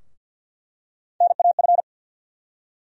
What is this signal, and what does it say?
Morse code, heard as a keyed tone, about 46 wpm, 700 Hz. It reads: NAF
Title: USA Call Sign Postfixes - 3 Letters 50wpm